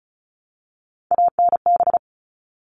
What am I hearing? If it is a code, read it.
Morse code, heard as a keyed tone, about 34 wpm, 700 Hz. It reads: ANB
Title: All Cut Numbers - 3-Digit 35wpm